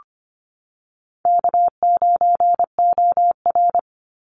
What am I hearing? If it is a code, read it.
Morse code, heard as a keyed tone, about 24 wpm, 700 Hz. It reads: K9OR
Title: Easy Call Signs - 25wpm